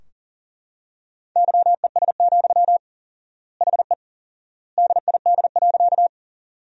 Morse code, heard as a keyed tone, about 40 wpm, 700 Hz. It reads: YES, HE DID.
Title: 4x Word-Spacing - Sentences from Top 500 Words - 40wpm